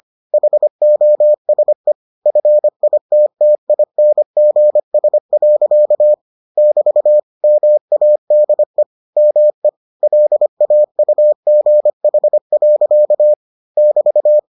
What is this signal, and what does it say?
Morse code, heard as a keyed tone, about 25 wpm, 600 Hz. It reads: HOSE FITTINGS. <BT> MADE ME LAUGH. <BT>